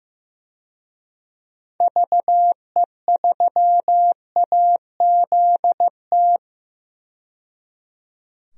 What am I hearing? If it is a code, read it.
Morse code, heard as a keyed tone, about 16 wpm, 700 Hz. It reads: VE3AZT
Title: DXCC-100 Call Signs - 6 Characters - 15wpm